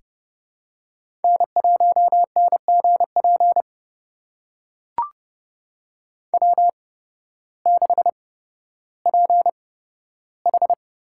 Morse code, heard as a keyed tone, about 30 wpm, spaced as for 14 wpm, 700 Hz. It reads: N1NGP W6PH
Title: USA Call Signs - 8x Character Spacing - 30wpm